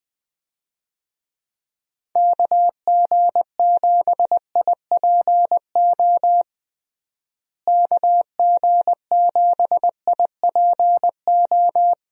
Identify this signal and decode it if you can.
Morse code, heard as a keyed tone, about 20 wpm, 700 Hz. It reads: KG7IPO KG7IPO
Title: USA Call Signs - 6 Characters - Repeated 2x - 20wpm